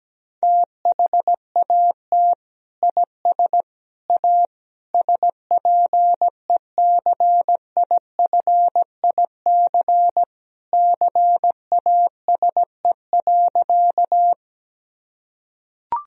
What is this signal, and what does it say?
Morse code, heard as a keyed tone, about 17 wpm, 700 Hz. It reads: THAT IS A SPECIFIC CASE.
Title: Sentences from Top 1000 Words - Review Part 3 - 17wpm